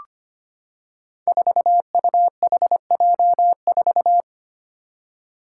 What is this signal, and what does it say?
Morse code, heard as a keyed tone, about 25 wpm, 700 Hz. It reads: 4UHJ4
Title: Five Challenging Letter-Number Combinations 25wpm